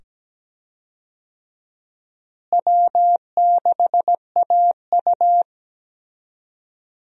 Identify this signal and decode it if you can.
Morse code, heard as a keyed tone, about 17 wpm, 700 Hz. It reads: W6AU
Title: USA Call Signs - 4 Characters - Repeated 2x - 17wpm